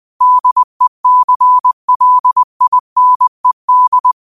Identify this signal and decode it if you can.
Morse code, heard as a keyed tone, about 20 wpm, 1000 Hz. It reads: DECLINED
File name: qotdfiles / 20wpm